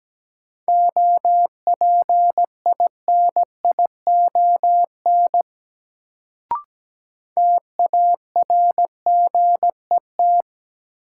Morse code, heard as a keyed tone, about 17 wpm, 700 Hz. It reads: OPINION TARGET
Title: Sentences from Top 1200 Words - New Word Drill Alt - 17wpm